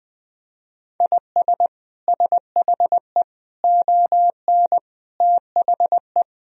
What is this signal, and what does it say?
Morse code, heard as a keyed tone, about 20 wpm, 700 Hz. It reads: IS SHE ON THE
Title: Sentences from Top 1000 Words - Review Part 3 - 20wpm